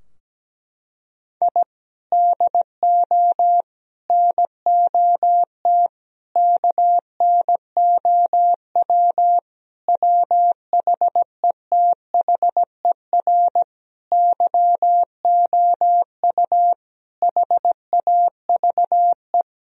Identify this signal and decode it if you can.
Morse code, heard as a keyed tone, about 17 wpm, 700 Hz. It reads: I DO NOT KNOW WHETHER YOU HAVE
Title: Sentences from Top 1000 Words - Review Part 3 - 17wpm